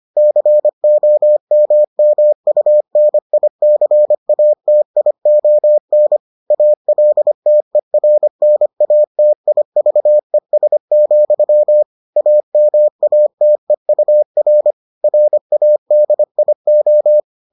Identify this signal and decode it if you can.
Morse code, heard as a keyed tone, about 25 wpm, 600 Hz. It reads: COMMUNICATION ALTERNATIVES, AMATEUR RADIO